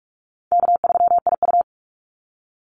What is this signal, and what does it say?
Morse code, heard as a keyed tone, about 44 wpm, 700 Hz. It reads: K3IU
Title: Call Signs Repeated 2x - 45wpm